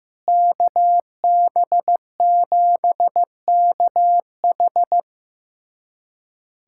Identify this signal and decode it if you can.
Morse code, heard as a keyed tone, about 16 wpm, 700 Hz. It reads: KB7KH
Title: USA Callsign DE Callsign 15wpm